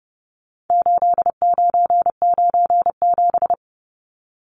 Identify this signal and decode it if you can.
Morse code, heard as a keyed tone, about 30 wpm, 700 Hz. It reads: 8997
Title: Numbers - 4-Digit 30wpm